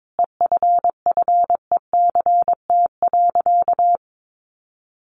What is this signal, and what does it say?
Morse code, heard as a keyed tone, about 22 wpm, 700 Hz. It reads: EFFECT.
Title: Sentences from Top 1400 Words 22wpm